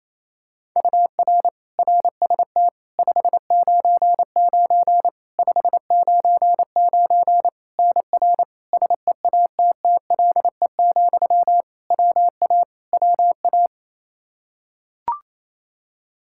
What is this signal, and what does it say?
Morse code, heard as a keyed tone, about 28 wpm, 700 Hz. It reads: UR RST 599 599 NR SEATTLE, WA WA
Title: Practice QSO - Exchange 1 - Beginning - 28wpm